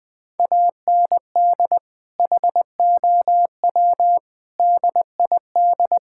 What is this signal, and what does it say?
Morse code, heard as a keyed tone, about 20 wpm, 700 Hz. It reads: AND HOW DID
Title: Sentences from Top 1000 Words - Review Part 6 - 20wpm